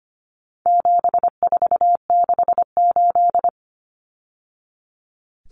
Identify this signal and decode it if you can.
Morse code, heard as a keyed tone, about 25 wpm, 700 Hz. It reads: 7468
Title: Numbers - 4-Digit 25wpm